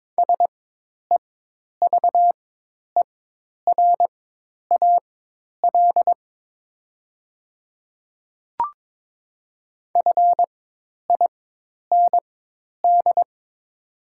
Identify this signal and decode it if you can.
Morse code, heard as a keyed tone, about 22 wpm, spaced as for 10 wpm, 700 Hz. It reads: SEVERAL FIND
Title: 4x Character Spacing - Top 500 Words - 22wpm